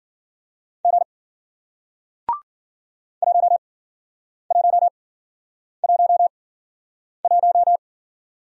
Morse code, heard as a keyed tone, about 52 wpm, spaced as for 9 wpm, 700 Hz. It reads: G 1111